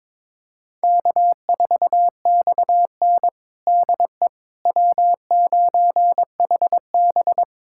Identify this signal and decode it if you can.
Morse code, heard as a keyed tone, about 22 wpm, 700 Hz. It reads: K4XN DE W9HB
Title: USA Callsign DE Callsign - 4 Characters - 22wpm